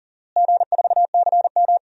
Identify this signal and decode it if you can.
Morse code, heard as a keyed tone, about 40 wpm, 700 Hz. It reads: G4CM
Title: DXCC-100 Call Signs - 4 Characters - Repeated 3x - 40wpm